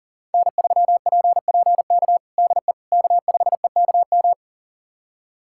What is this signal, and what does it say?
Morse code, heard as a keyed tone, about 37 wpm, 700 Hz. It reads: N3PPK DE K5EKM
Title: USA Callsign DE Callsign - 5 Characters - 40wpm